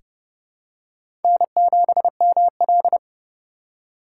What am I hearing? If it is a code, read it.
Morse code, heard as a keyed tone, about 31 wpm, 700 Hz. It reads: N7ML
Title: K1USN SST QSO Practice - 30wpm